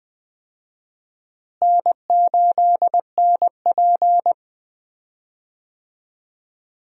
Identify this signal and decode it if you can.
Morse code, heard as a keyed tone, about 19 wpm, 700 Hz. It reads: N8NP
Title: USA Call Signs - 4 Characters - Repeated 3x - 20wpm